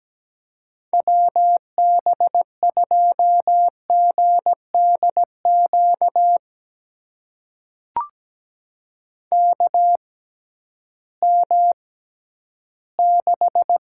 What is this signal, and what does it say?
Morse code, heard as a keyed tone, about 17 wpm, spaced as for 11 wpm, 700 Hz. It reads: WB2GDQ KM6
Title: USA Call Signs - 6 Characters - 6x Character Spacing - 17wpm